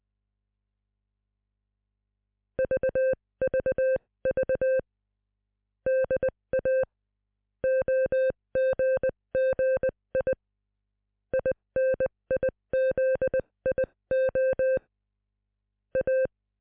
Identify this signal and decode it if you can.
Morse code, heard as a keyed tone, about 20 wpm, spaced as for 17 wpm, 520 Hz. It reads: VVV DA OGGI INIZIO A